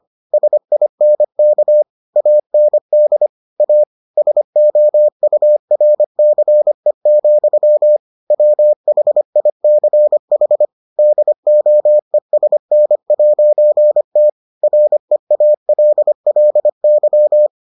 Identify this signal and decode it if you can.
Morse code, heard as a keyed tone, about 25 wpm, 600 Hz. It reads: SINK AND A SOURCE, WHICH DOESN'T REALLY